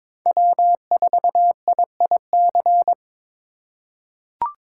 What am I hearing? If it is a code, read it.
Morse code, heard as a keyed tone, about 22 wpm, 700 Hz. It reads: W4IIC
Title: K1USN SST QSO Practice - 22wpm